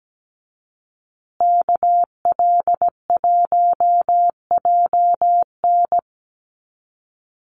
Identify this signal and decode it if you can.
Morse code, heard as a keyed tone, about 17 wpm, 700 Hz. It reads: KL1JN
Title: USA Call Signs Repeated 3x 17wpm